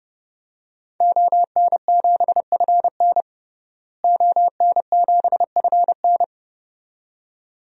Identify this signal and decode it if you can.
Morse code, heard as a keyed tone, about 30 wpm, 700 Hz. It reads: ON7FN ON7FN
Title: DXCC-100 Call Signs - 5 Characters - Repeated 2x - 30wpm